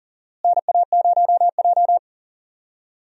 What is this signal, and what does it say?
Morse code, heard as a keyed tone, about 40 wpm, 700 Hz. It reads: NA0J
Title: USA Call Signs - 4 Characters - Repeated 2x - 40wpm